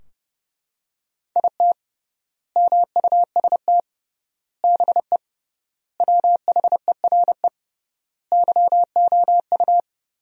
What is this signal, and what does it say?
Morse code, heard as a keyed tone, about 32 wpm, 700 Hz. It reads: IT MUST BE WHERE YOU
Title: 3x Word-Spacing - Sentences from Top 200 Words - 30wpm